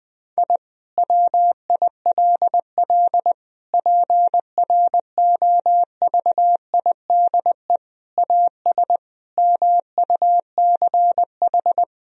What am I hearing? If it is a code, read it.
Morse code, heard as a keyed tone, about 20 wpm, 700 Hz. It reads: I WILL PROVIDE AS MUCH
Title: Sentences from Top 1000 Words - Review Part 3 - 20wpm